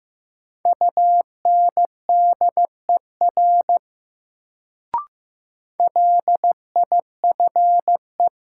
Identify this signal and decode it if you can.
Morse code, heard as a keyed tone, about 15 wpm, 700 Hz. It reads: UNDER LIFE
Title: Sentences from Top 300 Words - New Word Drill - 15wpm